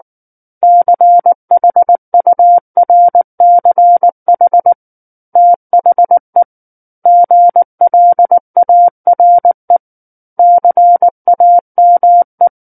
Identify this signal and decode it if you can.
Morse code, heard as a keyed tone, about 19 wpm, 700 Hz. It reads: CHURCH THE GLARE CAME
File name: Book One / 10-Chapter 10 - 19 WPM